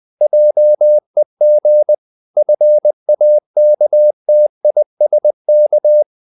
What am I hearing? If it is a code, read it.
Morse code, heard as a keyed tone, about 20 wpm, 600 Hz. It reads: JEG FAKTISK
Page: Ferie 20 wpm